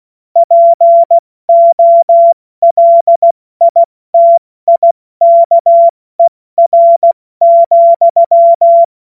Morse code, heard as a keyed tone, about 16 wpm, 680 Hz. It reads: POLITIKER,
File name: Witz-23-September-2025-70-F80